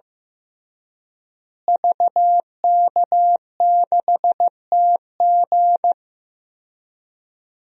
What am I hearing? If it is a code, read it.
Morse code, heard as a keyed tone, about 16 wpm, 700 Hz. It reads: VK6TG
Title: DXCC-100 Call Signs - 5 Characters - Repeated 3x - 15wpm